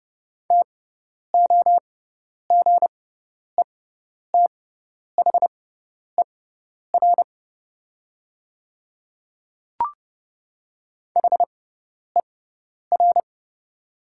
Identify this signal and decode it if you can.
Morse code, heard as a keyed tone, about 30 wpm, spaced as for 10 wpm, 690 Hz. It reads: TOGETHER HER
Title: 6x Character Spacing - Top 300 Words - 30wpm